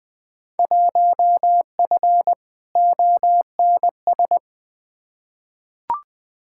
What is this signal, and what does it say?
Morse code, heard as a keyed tone, about 20 wpm, 700 Hz. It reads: 1F ONS
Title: Field Day Exchanges Encore - 20wpm